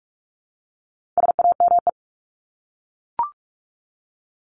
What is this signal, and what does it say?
Morse code, heard as a keyed tone, about 44 wpm, 700 Hz. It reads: SAME
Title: Top 300 Words 45wpm